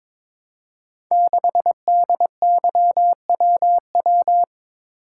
Morse code, heard as a keyed tone, about 22 wpm, 700 Hz. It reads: 6DYWW
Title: Code Groups - 5 Letter:Numbers - 22wpm